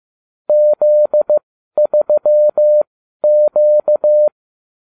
Morse code, heard as a keyed tone, about 15 wpm, spaced as for 12 wpm, 600 Hz. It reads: Z3Q